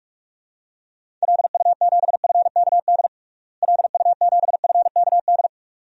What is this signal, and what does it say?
Morse code, heard as a keyed tone, about 42 wpm, 700 Hz. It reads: LU7FKD LU7FKD
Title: DXCC-100 Call Signs - 6 Characters - Repeated 2x - 45wpm